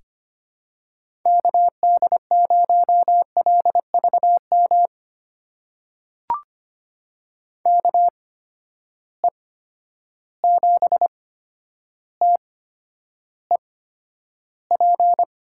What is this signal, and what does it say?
Morse code, heard as a keyed tone, about 25 wpm, spaced as for 11 wpm, 700 Hz. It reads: KD0LVM KE7TEP
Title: USA Call Signs - 6 Characters - 8x Character Spacing - 25wpm